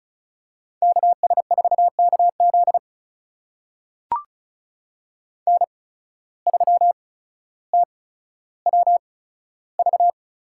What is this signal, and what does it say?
Morse code, heard as a keyed tone, about 36 wpm, spaced as for 15 wpm, 700 Hz. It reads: KS4KZ N3TWV
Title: USA Call Signs - 5 Characters - 8x Character Spacing - 35wpm